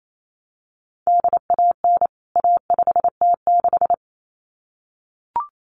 Morse code, heard as a keyed tone, about 28 wpm, 700 Hz. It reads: DAN A5T6
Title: CWT Exchange 28wpm